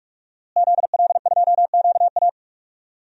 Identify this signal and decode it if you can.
Morse code, heard as a keyed tone, about 44 wpm, 700 Hz. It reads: ZL2QA
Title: DXCC-100 Call Signs - 5 Characters - Repeated 3x - 45wpm